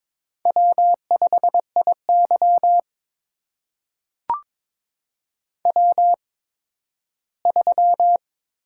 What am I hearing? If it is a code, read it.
Morse code, heard as a keyed tone, about 22 wpm, spaced as for 13 wpm, 700 Hz. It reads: W5IY W3
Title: USA Call Signs - 4 Characters - 8x Character Spacing - 22wpm